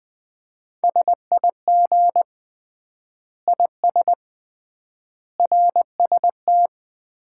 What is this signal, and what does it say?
Morse code, heard as a keyed tone, about 21 wpm, 700 Hz. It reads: SIG IS RST
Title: Common QSO Phrases - 3x Word-Spacing - 20wpm